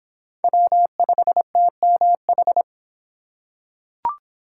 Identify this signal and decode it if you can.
Morse code, heard as a keyed tone, about 23 wpm, 700 Hz. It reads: W5TMH